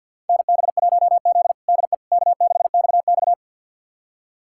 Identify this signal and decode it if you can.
Morse code, heard as a keyed tone, about 53 wpm, 690 Hz. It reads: NB1Z DE K6XX
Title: USA Callsign DE Callsign - 4 Characters - 50wpm